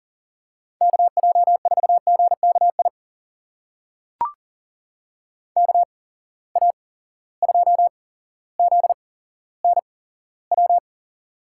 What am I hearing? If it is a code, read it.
Morse code, heard as a keyed tone, about 40 wpm, spaced as for 17 wpm, 700 Hz. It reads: KJ4GKI KA2ZNW
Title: USA Call Signs - 6 Characters - 8x Character Spacing - 40wpm